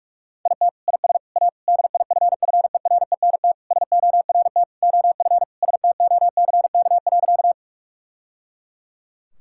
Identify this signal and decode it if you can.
Morse code, heard as a keyed tone, about 46 wpm, 700 Hz. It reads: IT IS A DIFFERENT SORT OF STOCK.
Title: Sentences from Top 1000 Words - Review Part 5 - 45wpm